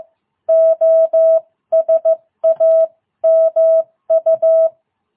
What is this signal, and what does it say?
Morse code, heard as a keyed tone, about 15 wpm, 660 Hz. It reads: OSAMU